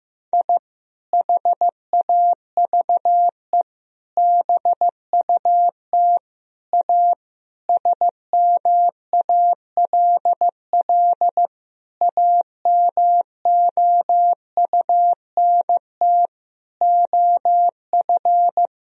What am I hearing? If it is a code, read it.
Morse code, heard as a keyed tone, about 16 wpm, 700 Hz. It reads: I HAVE BUT A SMALL AMOUNT OF
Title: Sentences from Top 1000 Words - Review Part 4 - 15wpm